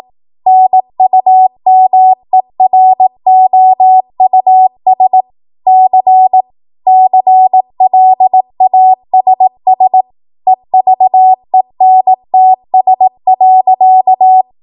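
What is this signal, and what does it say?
Morse code, heard as a keyed tone, about 18 wpm, 750 Hz. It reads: NUMEROUS C CLASS EVENTS.
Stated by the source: ARRL 18 wpm Morse code practice archive transmission.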